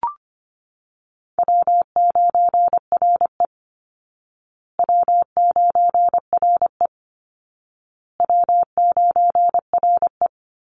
Morse code, heard as a keyed tone, about 25 wpm, 700 Hz. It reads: W9RE W9RE W9RE
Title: Easy Call Signs Repeated 3x - 25wpm